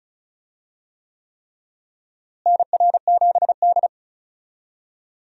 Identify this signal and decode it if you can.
Morse code, heard as a keyed tone, about 36 wpm, 700 Hz. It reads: NR7D
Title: USA Call Signs - 4 Characters - Repeated 3x - 35wpm